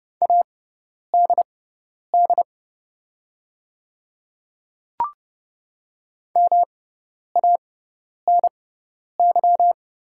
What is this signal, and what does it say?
Morse code, heard as a keyed tone, about 30 wpm, spaced as for 10 wpm, 690 Hz. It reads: ADD MANY
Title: 6x Character Spacing - Top 300 Words - 30wpm